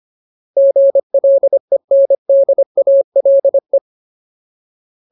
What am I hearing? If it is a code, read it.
Morse code, heard as a keyed tone, about 25 wpm, 550 Hz. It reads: GLENDALE
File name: Ultimate Word List / 25WPM Ultimate